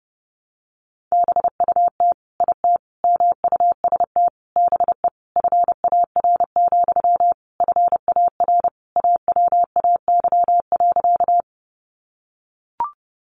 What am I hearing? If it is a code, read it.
Morse code, heard as a keyed tone, about 30 wpm, 700 Hz. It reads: BUT IT MUST BE FAR, FAR AWAY.
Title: Sentences from Top 600 Words 30wpm